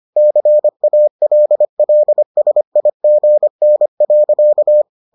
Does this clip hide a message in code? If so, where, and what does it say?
Yes: Morse code from 0.2 to 4.8 s, heard as a keyed tone, about 25 wpm, 600 Hz. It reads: CALLSIGN.